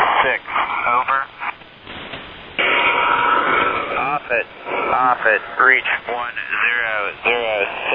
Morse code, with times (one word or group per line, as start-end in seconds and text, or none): none